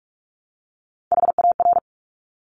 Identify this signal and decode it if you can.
Morse code, heard as a keyed tone, about 46 wpm, 690 Hz. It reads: HAR
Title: Top Three Letter Combinations in Words 45wpm